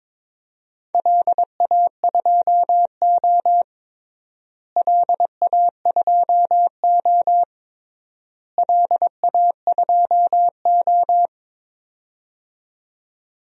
Morse code, heard as a keyed tone, about 22 wpm, 700 Hz. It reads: LA2O LA2O LA2O
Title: DXCC-100 Call Signs - 4 Characters - Repeated 3x - 22wpm